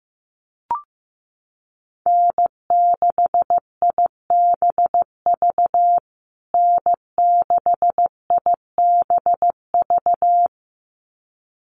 Morse code, heard as a keyed tone, about 15 wpm, 700 Hz. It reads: N6IBV N6IBV
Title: USA Call Signs Repeated 2x 15wpm